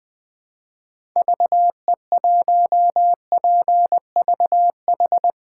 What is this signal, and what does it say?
Morse code, heard as a keyed tone, about 21 wpm, 700 Hz. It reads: VE1PVH
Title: 2023 SOTA Chaser Call Signs - 20wpm